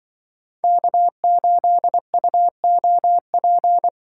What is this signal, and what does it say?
Morse code, heard as a keyed tone, about 25 wpm, 700 Hz. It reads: K8UOP
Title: Speed Racing - USA 1x3 Call Signs 20wpm